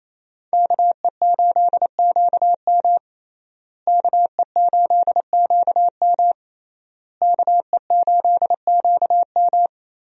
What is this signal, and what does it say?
Morse code, heard as a keyed tone, about 28 wpm, 700 Hz. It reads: KE8QM KE8QM KE8QM
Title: USA Call Signs - 5 Characters - Repeated 3x - 28wpm